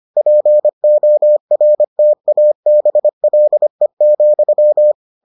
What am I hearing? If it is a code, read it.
Morse code, heard as a keyed tone, about 25 wpm, 600 Hz. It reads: PORTABLE,